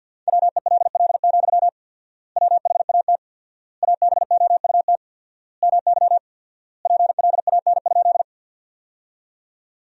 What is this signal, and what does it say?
Morse code, heard as a keyed tone, about 52 wpm, 700 Hz. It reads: WELL, WHAT ABOUT MY PLAN?
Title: 4x Word-Spacing - Sentences from Top 500 Words - 50wpm